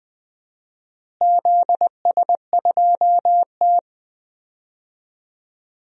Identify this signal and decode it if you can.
Morse code, heard as a keyed tone, about 20 wpm, 700 Hz. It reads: ZS2T
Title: DXCC-100 Call Signs - 4 Characters - Repeated 2x - 20wpm